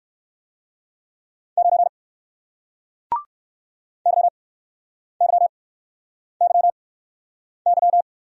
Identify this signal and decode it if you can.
Morse code, heard as a keyed tone, about 60 wpm, spaced as for 9 wpm, 700 Hz. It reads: 9 YYYY